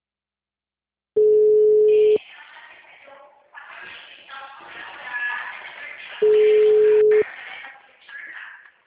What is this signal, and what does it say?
Ringback tone (Europe, 425 Hz).
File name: out- -2045- - - .11701